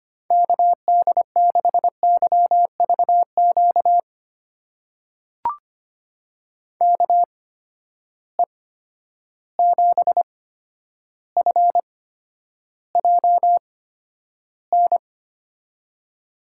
Morse code, heard as a keyed tone, about 25 wpm, spaced as for 11 wpm, 700 Hz. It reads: KD6YVQ KE7FJN
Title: USA Call Signs - 6 Characters - 8x Character Spacing - 25wpm